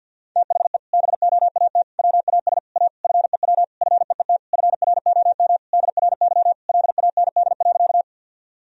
Morse code, heard as a keyed tone, about 50 wpm, 700 Hz. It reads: THE BOAT WAS A FEW FEET FROM DRY LAND.
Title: Sentences from Top 1000 Words - Review Part 3 - 50wpm